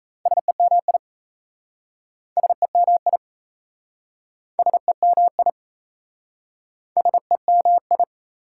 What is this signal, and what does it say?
Morse code, heard as a keyed tone, about 38 wpm, 690 Hz. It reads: SEMI SEMI SEMI SEMI